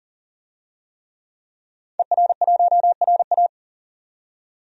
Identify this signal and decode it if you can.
Morse code, heard as a keyed tone, about 40 wpm, 700 Hz. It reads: ER1RA
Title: DXCC-100 Call Signs - 5 Characters - 40wpm